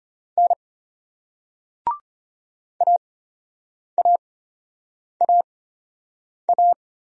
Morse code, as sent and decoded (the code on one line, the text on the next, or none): -. / .- .- .- .-
N AAAA